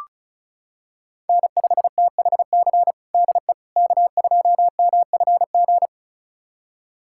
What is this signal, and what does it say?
Morse code, heard as a keyed tone, about 36 wpm, 700 Hz. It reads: N5THC DE K2MFG
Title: USA Callsign DE Callsign - 5 Characters - 35wpm